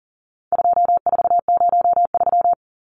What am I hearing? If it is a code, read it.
Morse code, heard as a keyed tone, about 40 wpm, 700 Hz. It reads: 2403
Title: Numbers - 4-Digit 40wpm